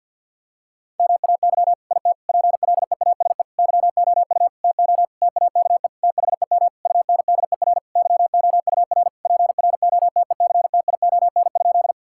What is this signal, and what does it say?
Morse code, heard as a keyed tone, about 50 wpm, 700 Hz. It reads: MAY IT PLEASE YOU TO TAKE THEM UNDER YOUR PROTECTION?
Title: Sentences from Top 1600 Words 50wpm